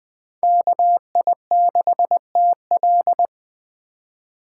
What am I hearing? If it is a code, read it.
Morse code, heard as a keyed tone, about 21 wpm, 700 Hz. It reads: KI6TL
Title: USA Call Signs - 5 Characters - 4x Character Spacing - 20wpm